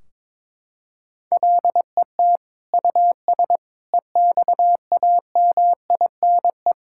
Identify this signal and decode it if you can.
Morse code, heard as a keyed tone, about 22 wpm, 700 Hz. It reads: LET US EXAMINE
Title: Sentences from Top 1600 Words 22wpm